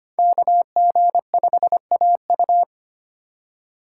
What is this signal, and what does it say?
Morse code, heard as a keyed tone, about 25 wpm, 700 Hz. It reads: KG5AU
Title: USA Call Signs - 5 Characters - 4x Character Spacing - 25wpm